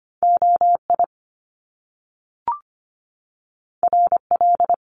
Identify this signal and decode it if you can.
Morse code, heard as a keyed tone, about 25 wpm, 700 Hz. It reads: OI RL
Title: Top Two Letter Combinations in Words 25wpm